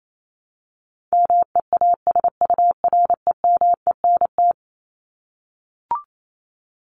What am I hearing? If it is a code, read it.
Morse code, heard as a keyed tone, about 28 wpm, 700 Hz. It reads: MEASUREMENT
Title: Top 1751-2000 Words 28wpm